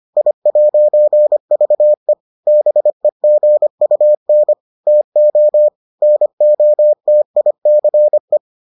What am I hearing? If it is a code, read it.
Morse code, heard as a keyed tone, about 25 wpm, 600 Hz. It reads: I'VE BEGUN TO NOTICE